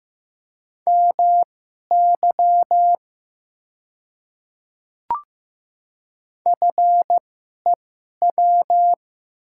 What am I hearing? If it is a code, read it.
Morse code, heard as a keyed tone, about 15 wpm, spaced as for 11 wpm, 700 Hz. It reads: MY FEW
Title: 2x Character Spacing - Top 300 Words - 15wpm